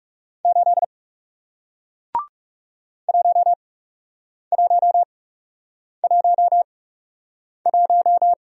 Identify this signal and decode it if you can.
Morse code, heard as a keyed tone, about 40 wpm, spaced as for 8 wpm, 700 Hz. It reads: 8 1111